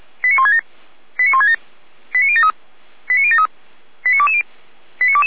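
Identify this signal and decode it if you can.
Five-tone selective call (ZVEI), heard as quick runs of five tones, tones of about 70 ms. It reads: 89167 89167 89073 89073 89220
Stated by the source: ZVEI five-tone selective call received on an SDR, from the Feuerwehr Orlamünde fire brigade server.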